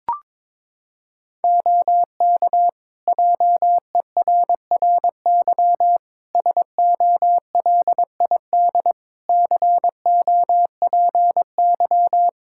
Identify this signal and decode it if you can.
Morse code, heard as a keyed tone, about 22 wpm, 700 Hz. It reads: OK JERRY SOLID COPY
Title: Practice QSO - Exchange 3 - More Info - 22wpm